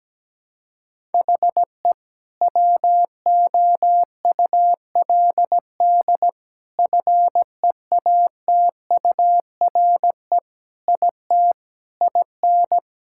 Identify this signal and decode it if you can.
Morse code, heard as a keyed tone, about 17 wpm, 700 Hz. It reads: HE WOULD FEATURE IT IN
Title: Sentences from Top 1000 Words - Review Part 5 - 17wpm